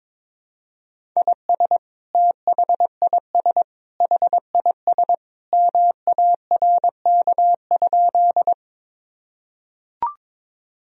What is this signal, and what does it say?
Morse code, heard as a keyed tone, about 21 wpm, 700 Hz. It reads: IS THIS HIS MARK?
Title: Sentences from Top 1000 Words - Review Part 1 - 22wpm